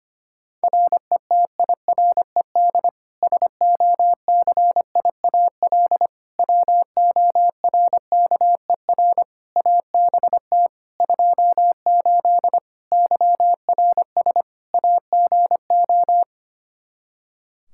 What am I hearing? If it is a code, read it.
Morse code, heard as a keyed tone, about 25 wpm, 700 Hz. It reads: RETIRED SOCIAL WORKER ABT 28 YRS AGO
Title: K1USN SST QSO Practice 25wpm